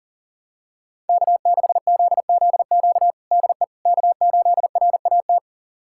Morse code, heard as a keyed tone, about 40 wpm, 700 Hz. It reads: K6ZZQ DE K8RAT
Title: USA Callsign DE Callsign 40wpm